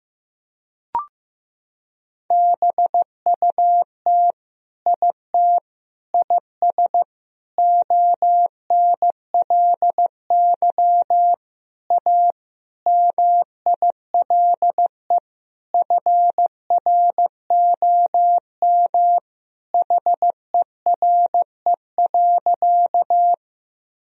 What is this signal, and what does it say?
Morse code, heard as a keyed tone, about 15 wpm, 700 Hz. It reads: BUT IT IS ONLY A MILE FROM HERE.
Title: Sentences from Top 1000 Words - Review Part 1 - 15wpm